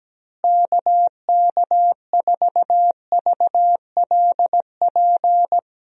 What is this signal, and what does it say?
Morse code, heard as a keyed tone, about 17 wpm, 700 Hz. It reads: KK4VLP
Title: USA Call Signs - 6 Characters - Repeated 3x - 17wpm